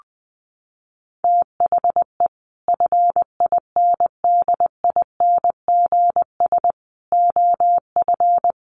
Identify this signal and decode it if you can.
Morse code, heard as a keyed tone, about 19 wpm, 700 Hz. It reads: THE FINDINGS OF
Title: Sets of 5 Words Encore2 20wpm